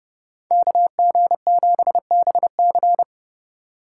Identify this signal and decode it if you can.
Morse code, heard as a keyed tone, about 30 wpm, 700 Hz. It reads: KG7BC